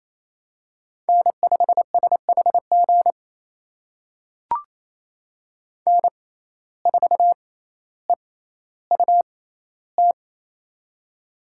Morse code, heard as a keyed tone, about 28 wpm, spaced as for 14 wpm, 700 Hz. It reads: N5SHG N4EUT
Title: USA Call Signs - 5 Characters - 6x Character Spacing - 28wpm